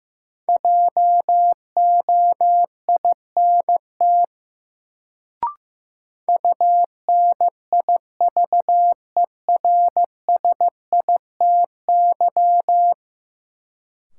Morse code, heard as a keyed tone, about 15 wpm, 700 Hz. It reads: JOINT UNIVERSITY
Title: Sentences from Top 1300 Words - New Word Drill Alt - 15wpm